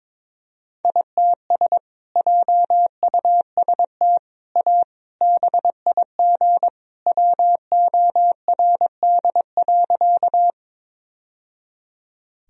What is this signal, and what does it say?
Morse code, heard as a keyed tone, about 22 wpm, 700 Hz. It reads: ITS JUST A BIG WORD.
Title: Sentences from Top 1000 Words - Review Part 1 - 22wpm